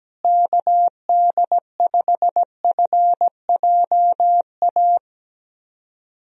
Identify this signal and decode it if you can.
Morse code, heard as a keyed tone, about 17 wpm, 700 Hz. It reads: KD5FJA
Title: USA Call Signs - 6 Characters - Repeated 2x - 17wpm